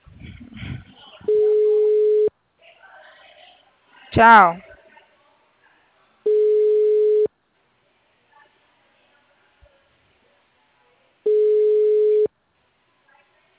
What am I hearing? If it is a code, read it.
Ringback tone (Europe, 425 Hz), with speech.